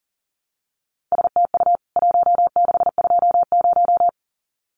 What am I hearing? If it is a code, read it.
Morse code, heard as a keyed tone, about 40 wpm, 700 Hz. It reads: STU 1620
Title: CWT Exchange 40wpm